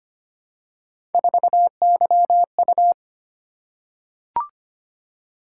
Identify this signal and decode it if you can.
Morse code, heard as a keyed tone, about 25 wpm, 700 Hz. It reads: 4YU
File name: Three Letter-Number Combinations - 25wpm